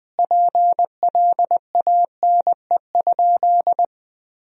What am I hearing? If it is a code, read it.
Morse code, heard as a keyed tone, about 20 wpm, 700 Hz. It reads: PLANE?
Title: Sentences from Top 1000 Words - Review Part 3 - 20wpm